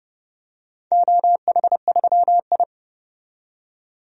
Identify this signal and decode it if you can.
Morse code, heard as a keyed tone, about 30 wpm, 700 Hz. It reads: OH3I
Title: DXCC-100 Call Signs - 4 Characters - 30wpm